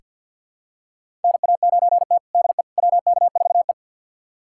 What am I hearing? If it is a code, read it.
Morse code, heard as a keyed tone, about 49 wpm, 700 Hz. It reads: NA9T DE WK4E
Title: USA Callsign DE Callsign - 4 Characters - 50wpm